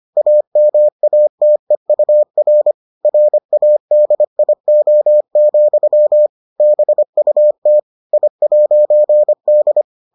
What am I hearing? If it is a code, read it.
Morse code, heard as a keyed tone, about 25 wpm, 600 Hz. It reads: AMATEUR RADIO, BUT I'D